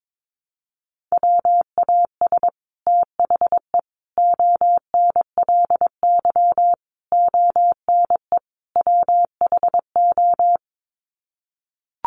Morse code, heard as a keyed tone, about 22 wpm, 700 Hz. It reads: WAS THE ONLY ONE WHO
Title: Sets of 5 Words Encore2 22wpm